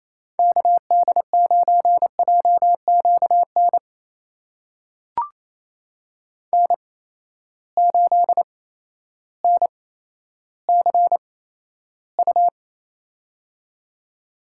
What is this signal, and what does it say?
Morse code, heard as a keyed tone, about 28 wpm, spaced as for 13 wpm, 700 Hz. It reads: KD9JQN N8NCU
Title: USA Call Signs - 8x Character Spacing - 28wpm